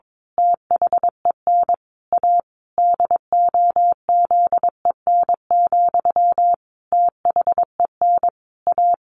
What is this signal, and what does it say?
Morse code, heard as a keyed tone, about 22 wpm, 700 Hz. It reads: THEN A DOZEN, THEN A
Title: A Sound Of Thunder 22wpm